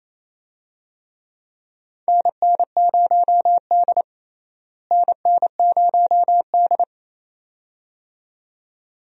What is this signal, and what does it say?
Morse code, heard as a keyed tone, about 28 wpm, 700 Hz. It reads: NN0D NN0D
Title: USA Call Signs - 4 Characters - Repeated 2x - 28wpm